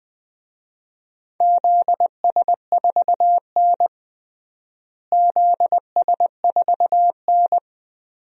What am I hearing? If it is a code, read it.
Morse code, heard as a keyed tone, about 20 wpm, 690 Hz. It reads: ZS4N ZS4N
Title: DXCC-100 Call Signs - 4 Characters - Repeated 3x - 20wpm